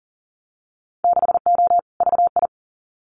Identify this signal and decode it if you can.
Morse code, heard as a keyed tone, about 43 wpm, 700 Hz. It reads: 6O VI
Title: Winter Field Day Exchanges Encore 40wpm